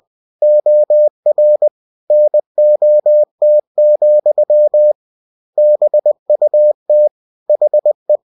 Morse code, heard as a keyed tone, about 20 wpm, 600 Hz. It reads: OR NOT, BUT HE
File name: cw-books / winnie-the-pooh / winnie-20wpm-page-0104